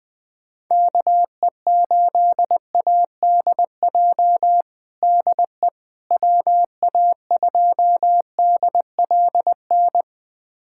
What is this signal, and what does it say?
Morse code, heard as a keyed tone, about 21 wpm, 700 Hz. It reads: KE8ADJ DE WA2DLN
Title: USA Callsign DE Callsign - 6 Characters - 20wpm